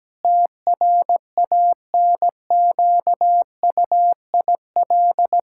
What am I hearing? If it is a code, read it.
Morse code, heard as a keyed tone, about 17 wpm, 700 Hz. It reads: TRANQUIL
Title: Pangram Challenge v2 - 17wpm